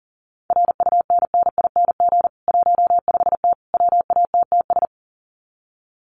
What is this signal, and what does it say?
Morse code, heard as a keyed tone, about 40 wpm, 700 Hz. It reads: RUNNING 15T WATTS
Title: Common QSO Phrases 40wpm